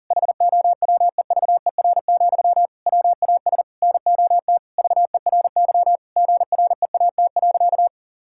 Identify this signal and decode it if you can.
Morse code, heard as a keyed tone, about 40 wpm, 700 Hz. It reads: HOWEVER, WAS NOT VERY GREAT.
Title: Sentences from Top 1000 Words - Review Part 5 - 40wpm